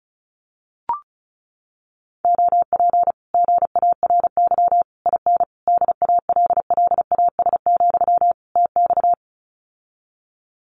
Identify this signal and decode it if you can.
Morse code, heard as a keyed tone, about 36 wpm, 700 Hz. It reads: OP GARY IN DALLAS, TX
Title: Common QSO Phrases Encore 35wpm